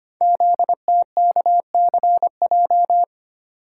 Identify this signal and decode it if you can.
Morse code, heard as a keyed tone, about 25 wpm, 700 Hz. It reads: ZTKCJ
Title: Code Groups - 5 Letters - 25wpm